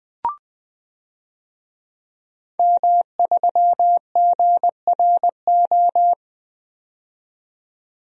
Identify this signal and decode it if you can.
Morse code, heard as a keyed tone, about 20 wpm, 700 Hz. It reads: M3GRO
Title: DXCC-100 Call Signs - 5 Characters - 20wpm